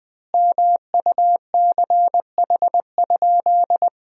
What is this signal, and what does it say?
Morse code, heard as a keyed tone, about 20 wpm, 700 Hz. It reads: MUCH?
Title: 3x Word-Spacing - Sentences from Top 400 Words - 20wpm